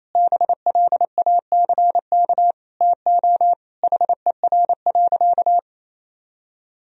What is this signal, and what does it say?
Morse code, heard as a keyed tone, about 28 wpm, 700 Hz. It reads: BLACK TO HER.
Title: Sentences from Top 1000 Words - Review Part 2 - 28wpm